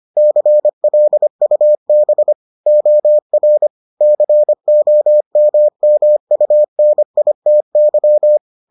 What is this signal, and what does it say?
Morse code, heard as a keyed tone, about 25 wpm, 600 Hz. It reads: CLUB OR COMMUNITY